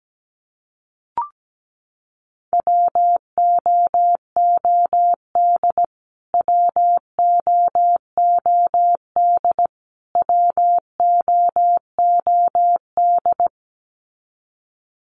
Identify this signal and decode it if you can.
Morse code, heard as a keyed tone, about 17 wpm, 700 Hz. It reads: WOOD WOOD WOOD
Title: Top 500 Words Repeated 17wpm